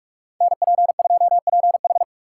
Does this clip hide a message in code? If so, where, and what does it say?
Yes: Morse code from 0.4 to 2.0 s, heard as a keyed tone, about 48 wpm, 700 Hz. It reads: NP2PH